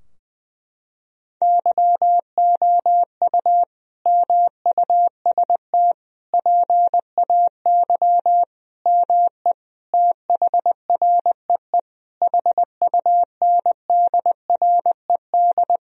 Morse code, heard as a keyed tone, about 21 wpm, 700 Hz. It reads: YOU MUST PAY ME THREE HUNDRED
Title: Sentences from Top 1000 Words - Review Part 2 - 20wpm